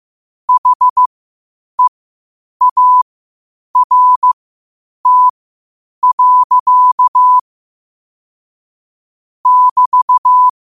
Morse code, heard as a keyed tone, about 15 wpm, spaced as for 8 wpm, 1000 Hz. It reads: HEART. <BT>
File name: qotdfiles / 8wpm